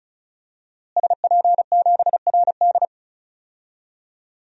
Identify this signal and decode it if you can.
Morse code, heard as a keyed tone, about 34 wpm, 700 Hz. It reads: SP7RD
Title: DXCC-100 Call Signs - 5 Characters - Repeated 2x - 35wpm